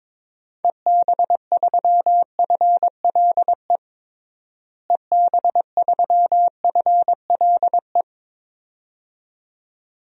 Morse code, heard as a keyed tone, about 22 wpm, 700 Hz. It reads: EB3FLE EB3FLE
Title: DXCC-100 Call Signs - 6 Characters - Repeated 2x - 22wpm